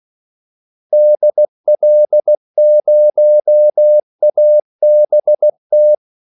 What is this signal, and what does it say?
Morse code, heard as a keyed tone, about 16 wpm, 600 Hz. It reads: DL0ABT
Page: Kaldesignaler 16 wpm